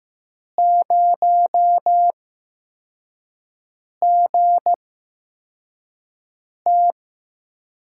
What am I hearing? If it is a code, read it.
Morse code, heard as a keyed tone, about 15 wpm, spaced as for 4 wpm, 700 Hz. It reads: 0GT
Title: USA Call Signs - 8x Character Spacing - 15wpm